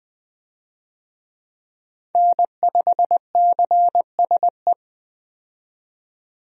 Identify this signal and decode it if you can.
Morse code, heard as a keyed tone, about 21 wpm, 700 Hz. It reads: N5CSE
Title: USA Call Signs - 5 Characters - 20wpm